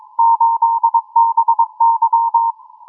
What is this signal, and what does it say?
Morse code, heard as a keyed tone, about 22 wpm, 920 Hz. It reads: 8BY